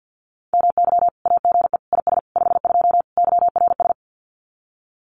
Morse code, heard as a keyed tone, about 50 wpm, 700 Hz. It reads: MY AGE IS 52 YRS
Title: Common QSO Phrases Encore 50wpm